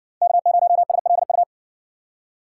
Morse code, heard as a keyed tone, about 60 wpm, 700 Hz. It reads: K0SLU